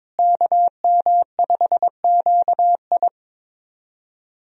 Morse code, heard as a keyed tone, about 22 wpm, 700 Hz. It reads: KM5QI
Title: USA Call Signs - 5 Characters - Repeated 3x - 22wpm